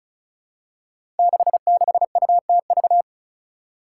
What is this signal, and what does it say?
Morse code, heard as a keyed tone, about 36 wpm, 700 Hz. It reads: 66UTV
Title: Five Challenging Letter-Number Combinations 35wpm